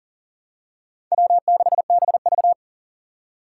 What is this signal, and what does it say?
Morse code, heard as a keyed tone, about 40 wpm, 690 Hz. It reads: W6BV
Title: Four Challenging Letter-Number Combinations 40wpm